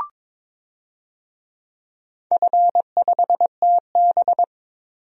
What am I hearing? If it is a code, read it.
Morse code, heard as a keyed tone, about 22 wpm, 700 Hz. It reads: F5TB
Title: DXCC-100 Call Signs - 4 Characters - 22wpm